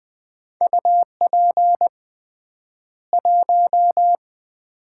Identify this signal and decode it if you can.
Morse code, heard as a keyed tone, about 20 wpm, 700 Hz. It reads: UP 1
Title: Common QSO Phrases - 3x Word-Spacing - 20wpm